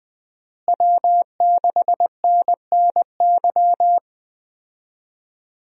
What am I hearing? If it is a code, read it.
Morse code, heard as a keyed tone, about 20 wpm, 700 Hz. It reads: W6NNY
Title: USA Call Signs - 5 Characters - Repeated 3x - 20wpm